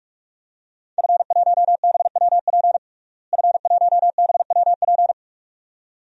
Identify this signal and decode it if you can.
Morse code, heard as a keyed tone, about 46 wpm, 700 Hz. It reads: F1BWP F1BWP
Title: DXCC-100 Call Signs - 5 Characters - Repeated 2x - 45wpm